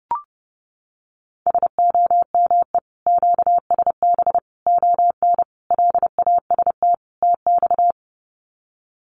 Morse code, heard as a keyed tone, about 30 wpm, 700 Hz. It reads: SOME QSB ON LAST TX
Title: Common QSO Phrases 30wpm